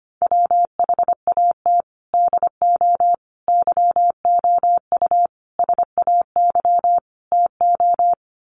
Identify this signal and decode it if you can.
Morse code, heard as a keyed tone, about 25 wpm, 700 Hz. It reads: WHAT DO YOU SAY TO
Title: Sets of 5 Words 25wpm